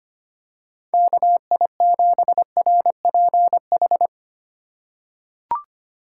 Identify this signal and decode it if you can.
Morse code, heard as a keyed tone, about 25 wpm, 700 Hz. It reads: KI7RPH
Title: USA Call Signs - 6 Characters - 2x Character Spacing - 25wpm